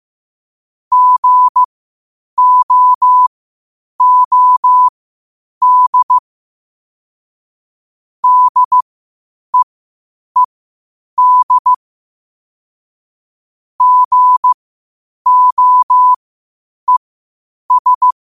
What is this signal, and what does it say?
Morse code, heard as a keyed tone, about 15 wpm, spaced as for 8 wpm, 1000 Hz. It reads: GOOD DEED GOES
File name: qotdfiles / 8wpm